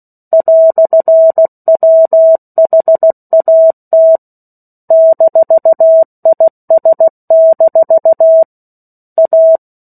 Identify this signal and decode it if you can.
Morse code, heard as a keyed tone, about 16 wpm, 650 Hz. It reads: "WHAT -IS- A
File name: 03-Chapter-03 16 WPM